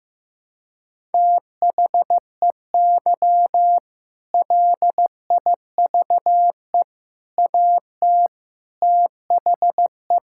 Morse code, heard as a keyed tone, about 15 wpm, 700 Hz. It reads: THEY LIVE AT THE
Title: Sentences from Top 1000 Words - Review Part 6 - 15wpm